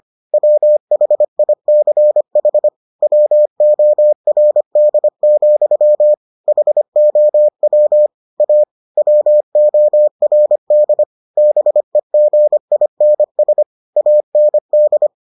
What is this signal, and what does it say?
Morse code, heard as a keyed tone, about 25 wpm, 600 Hz. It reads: WHICH WORD, HOW A WORD BEGINS AND